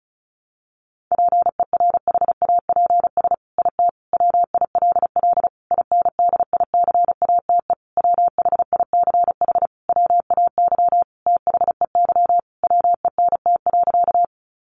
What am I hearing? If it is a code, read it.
Morse code, heard as a keyed tone, about 34 wpm, 700 Hz. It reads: PERHAPS IT WILL INDICATE WHICH WAY THEY WENT.
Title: Sentences from Top 1300 Words 35wpm